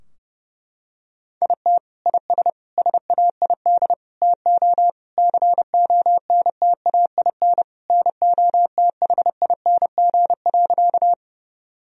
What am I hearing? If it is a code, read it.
Morse code, heard as a keyed tone, about 30 wpm, 700 Hz. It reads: IT IS SAID TO CONTAIN NOTHING.
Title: Sentences from Top 1000 Words - Review Part 2 - 30wpm